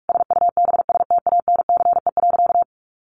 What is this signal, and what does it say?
Morse code, heard as a keyed tone, about 44 wpm, 700 Hz. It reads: SUBSTANCE.
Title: Sentences from Top 1500 Words 45wpm